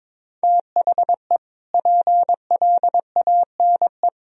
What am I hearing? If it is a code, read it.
Morse code, heard as a keyed tone, about 22 wpm, 700 Hz. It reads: THE PLANE
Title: Sentences from Top 1000 Words - Review Part 3 - 22wpm